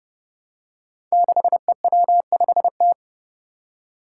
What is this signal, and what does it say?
Morse code, heard as a keyed tone, about 30 wpm, 700 Hz. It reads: 6EW5T
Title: Five Challenging Letter-Number Combinations 30wpm